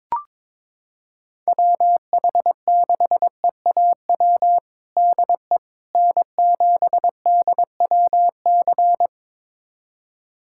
Morse code, heard as a keyed tone, about 22 wpm, 700 Hz. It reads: WH6EAW DE N7DWC
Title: USA Callsign DE Callsign 22wpm